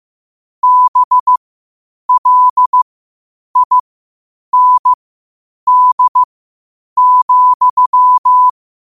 Morse code, heard as a keyed tone, about 15 wpm, spaced as for 8 wpm, 1000 Hz. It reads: BLIND,
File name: qotdfiles / 8wpm